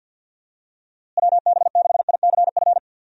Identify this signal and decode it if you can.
Morse code, heard as a keyed tone, about 48 wpm, 700 Hz. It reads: WB6ICF
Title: USA Call Signs - 6 Characters - Repeated 2x - 50wpm